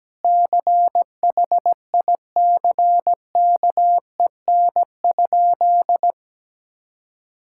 Morse code, heard as a keyed tone, about 17 wpm, 700 Hz. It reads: CHICKEN?
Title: Sentences from Top 1000 Words - Review Part 5 - 17wpm